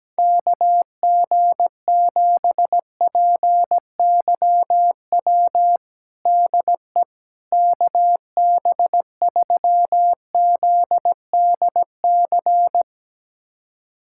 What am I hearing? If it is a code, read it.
Morse code, heard as a keyed tone, about 17 wpm, 700 Hz. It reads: KG7PYW DE KB3ZDC
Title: USA Callsign DE Callsign - 6 Characters - 17wpm